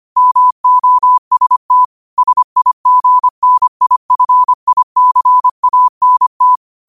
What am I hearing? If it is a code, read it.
Morse code, heard as a keyed tone, about 25 wpm, 1000 Hz. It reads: MOST SIGNIFICANT
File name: qotdfiles / 25wpm